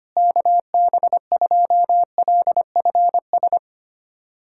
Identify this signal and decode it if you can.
Morse code, heard as a keyed tone, about 25 wpm, 700 Hz. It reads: KB2LFS
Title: USA Callsign DE Callsign - 6 Characters - 25wpm